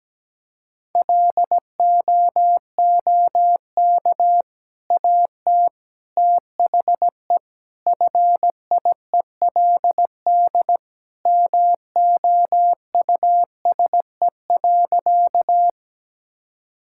Morse code, heard as a keyed tone, about 17 wpm, 700 Hz. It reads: LOOK AT THE FIELD MOUSE.
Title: Sentences from Top 1000 Words - Review Part 5 - 17wpm